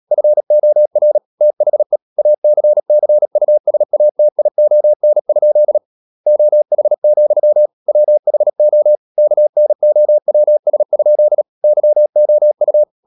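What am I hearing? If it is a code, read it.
Morse code, heard as a keyed tone, about 37 wpm, 600 Hz. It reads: FOR THE ACCUSATION? OH, WHO KNOWS? YOU
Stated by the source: Morse code practice at 35–39 wpm sending English text of Ken Follett's novel Never, from CW med Gnister.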